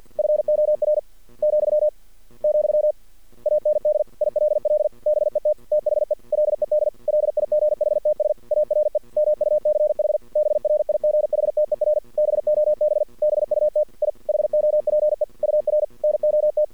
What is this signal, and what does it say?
Morse code, heard as a keyed tone, about 53 wpm, 600 Hz. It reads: FOR , , AND ILL BET THE LED LIGHTS ARE MUCH BRIGHTER BOB BUT I HOPE UR NOT
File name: archived / CRSnet